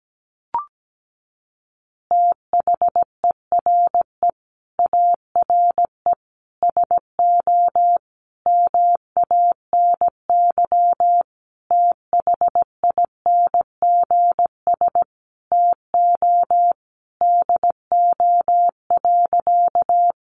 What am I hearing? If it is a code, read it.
Morse code, heard as a keyed tone, about 17 wpm, 700 Hz. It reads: THERE ARE SO MANY THINGS TO DO.
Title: Additional Easy Phrases 17wpm